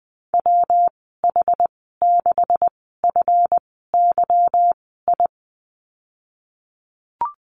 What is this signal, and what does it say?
Morse code, heard as a keyed tone, about 20 wpm, spaced as for 15 wpm, 700 Hz. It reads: WH6FYI
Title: 167 Lesson 27g - Number 7 - Call Signs 20wpm